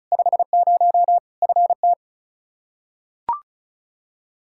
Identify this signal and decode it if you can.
Morse code, heard as a keyed tone, about 36 wpm, 700 Hz. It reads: 50 FT
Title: Practice QSO - Exchange 2 - Info - 35wpm